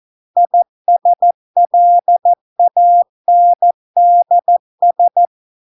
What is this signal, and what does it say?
Morse code, heard as a keyed tone, about 14 wpm, 700 Hz. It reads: ISLANDS